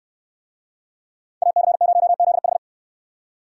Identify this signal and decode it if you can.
Morse code, heard as a keyed tone, about 63 wpm, 700 Hz. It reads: AC0ZH